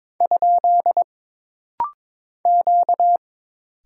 Morse code, heard as a keyed tone, about 22 wpm, 700 Hz. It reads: ? Q
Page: This rapid-fire timing is geared to push your ICR (Instant Character Recognition) skills.